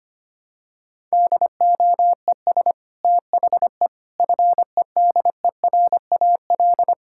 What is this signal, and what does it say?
Morse code, heard as a keyed tone, about 25 wpm, 700 Hz. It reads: DOES THE FEDERAL
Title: Sentences from Top 1000 Words - Review Part 5 - 25wpm